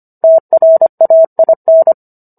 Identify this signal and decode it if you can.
Morse code, heard as a keyed tone, about 25 wpm, 650 Hz. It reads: TRAIN